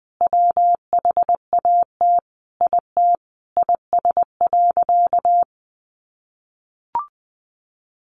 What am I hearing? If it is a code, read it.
Morse code, heard as a keyed tone, about 20 wpm, 700 Hz. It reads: WHAT IT IS.
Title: Sentences from Top 1500 Words 20wpm